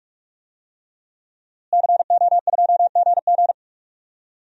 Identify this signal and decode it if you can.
Morse code, heard as a keyed tone, about 44 wpm, 700 Hz. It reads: CO2GG
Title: DXCC-100 Call Signs - 5 Characters - Repeated 3x - 45wpm